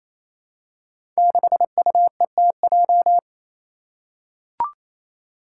Morse code, heard as a keyed tone, about 27 wpm, 700 Hz. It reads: 6UETJ
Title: Five Challenging Letter-Number Combinations 28wpm